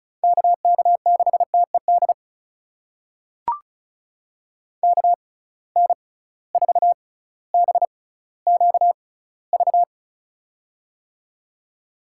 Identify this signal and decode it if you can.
Morse code, heard as a keyed tone, about 36 wpm, spaced as for 18 wpm, 700 Hz. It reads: KK6TED KN4BQV
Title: USA Call Signs - 6 Characters - 6x Character Spacing - 35wpm